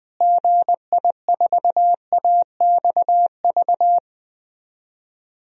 Morse code, heard as a keyed tone, about 21 wpm, 700 Hz. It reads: GI4AXV
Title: DXCC-100 Call Signs - 6 Characters - Repeated 3x - 20wpm